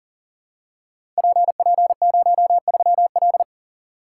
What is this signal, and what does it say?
Morse code, heard as a keyed tone, about 40 wpm, 700 Hz. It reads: PP03L
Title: Code Groups - 5 Letter:Numbers - 40wpm